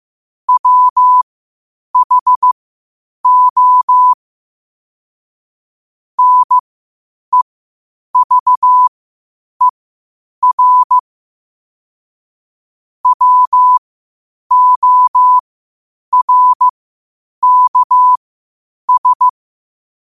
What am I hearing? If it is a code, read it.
Morse code, heard as a keyed tone, about 15 wpm, spaced as for 8 wpm, 1000 Hz. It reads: WHO NEVER WORKS
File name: qotdfiles / 8wpm